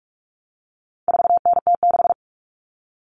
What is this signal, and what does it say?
Morse code, heard as a keyed tone, about 44 wpm, 700 Hz. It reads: 4NT6
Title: All Cut Numbers - 4-Digit 45wpm